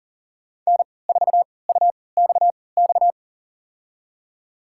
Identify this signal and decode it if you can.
Morse code, heard as a keyed tone, about 40 wpm, spaced as for 23 wpm, 700 Hz. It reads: N4UXX
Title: USA Call Signs - 3x Character Spacing - 40wpm